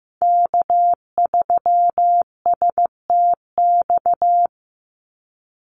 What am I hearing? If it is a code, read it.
Morse code, heard as a keyed tone, about 15 wpm, 700 Hz. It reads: K3STX
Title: Call Signs Repeated 3x - 15wpm